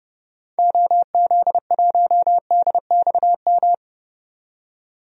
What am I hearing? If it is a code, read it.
Morse code, heard as a keyed tone, about 30 wpm, 700 Hz. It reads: OZ1DXM
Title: DXCC-100 Call Signs - 6 Characters - 30wpm